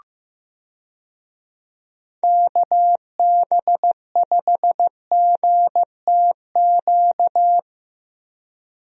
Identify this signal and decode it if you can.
Morse code, heard as a keyed tone, about 15 wpm, 700 Hz. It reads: KB5GTQ
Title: USA Call Signs - 6 Characters - Repeated 2x - 15wpm